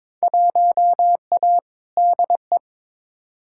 Morse code, heard as a keyed tone, about 21 wpm, 700 Hz. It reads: 1A DE
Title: Field Day Exchanges Encore - 22wpm